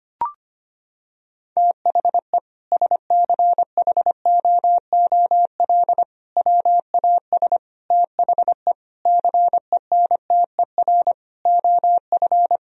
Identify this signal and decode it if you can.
Morse code, heard as a keyed tone, about 25 wpm, 700 Hz. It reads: THE SCHOOL WAS THE CENTER OF
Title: Sentences from Top 1000 Words - Review Part 2 - 25wpm